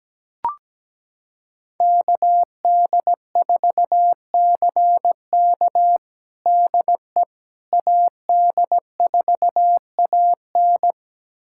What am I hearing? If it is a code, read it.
Morse code, heard as a keyed tone, about 17 wpm, 700 Hz. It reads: KD4CK DE AD4AN
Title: USA Callsign DE Callsign - 5 Characters - 17wpm